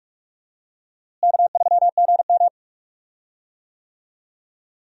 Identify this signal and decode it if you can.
Morse code, heard as a keyed tone, about 46 wpm, 700 Hz. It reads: K3GM
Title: USA Call Signs - 4 Characters - Repeated 2x - 45wpm